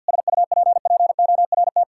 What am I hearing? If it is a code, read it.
Morse code, heard as a keyed tone, about 50 wpm, 700 Hz. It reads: SUPPORT
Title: Sentences from Top 1000 Words - Review Part 5 - 50wpm